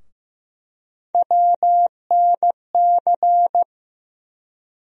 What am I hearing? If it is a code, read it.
Morse code, heard as a keyed tone, about 15 wpm, 700 Hz. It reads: WNC
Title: USA Call Sign Postfixes - 3 Letters 15wpm